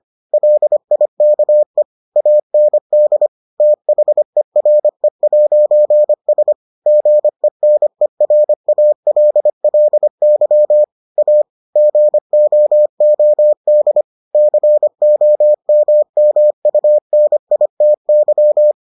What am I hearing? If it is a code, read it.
Morse code, heard as a keyed tone, about 25 wpm, 600 Hz. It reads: LIKE AND THERE'S GENERALLY A GOOD COMMUNITY